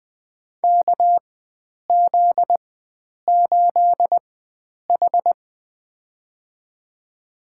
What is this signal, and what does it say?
Morse code, heard as a keyed tone, about 20 wpm, spaced as for 9 wpm, 700 Hz. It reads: KZ8H
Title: USA Call Signs - 4x Character Spacing - 20wpm